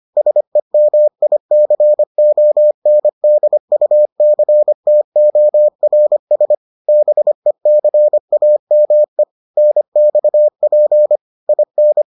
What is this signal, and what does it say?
Morse code, heard as a keyed tone, about 25 wpm, 600 Hz. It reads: SEMICONDUCTORS BECAME NXP IN